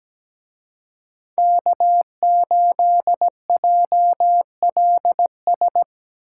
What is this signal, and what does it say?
Morse code, heard as a keyed tone, about 17 wpm, 700 Hz. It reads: K8JLS
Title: Speed Racing - USA Call Signs 17wpm